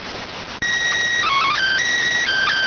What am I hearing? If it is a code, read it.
SSTV (slow-scan television), PD 290.